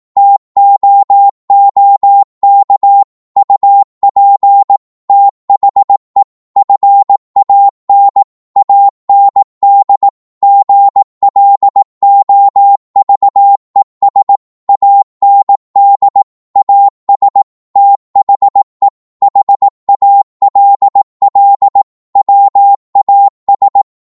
Morse code, heard as a keyed tone, about 18 wpm, 800 Hz. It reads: TOOK UP THE FAN AND GLOVES AND AS THE HALL WAS
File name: Morse / psg-18wpm